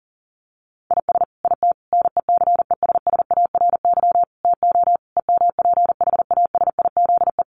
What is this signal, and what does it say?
Morse code, heard as a keyed tone, about 40 wpm, 700 Hz. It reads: IS IT NECESSARY TO EMPHASIZE
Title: Sentences from Top 1500 Words 40wpm